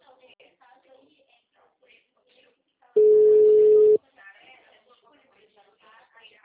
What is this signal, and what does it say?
Ringback tone (Europe, 425 Hz), with speech.
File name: out- -2013- - - .4599